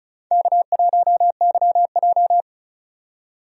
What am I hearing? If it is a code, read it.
Morse code, heard as a keyed tone, about 34 wpm, 700 Hz. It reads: K1YJ
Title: USA Call Signs - 4 Characters - 35wpm